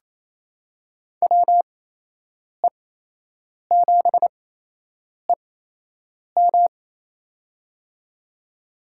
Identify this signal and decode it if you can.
Morse code, heard as a keyed tone, about 28 wpm, spaced as for 8 wpm, 700 Hz. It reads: WE7EM
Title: USA Call Signs - 5 Characters - 8x Character Spacing - 28wpm